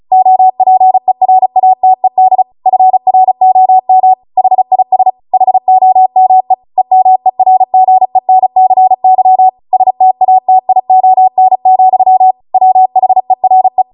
Morse code, heard as a keyed tone, about 36 wpm, 750 Hz. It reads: OPERATED FROM HIS HOME EMERGENCY STATION, WHERE
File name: Morse / 35WPM